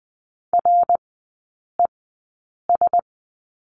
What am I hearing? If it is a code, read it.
Morse code, heard as a keyed tone, about 20 wpm, spaced as for 8 wpm, 700 Hz. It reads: RES
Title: 053 Lesson 10e - Letter r - Three Characters 20wpm